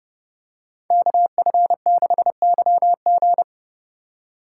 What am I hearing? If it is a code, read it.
Morse code, heard as a keyed tone, about 29 wpm, 700 Hz. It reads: KF6YG